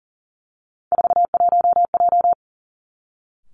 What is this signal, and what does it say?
Morse code, heard as a keyed tone, about 40 wpm, 700 Hz. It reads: 41J
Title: Three Challenging Letter-Number Combinations 40wpm